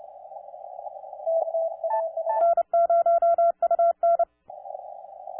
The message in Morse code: N0UN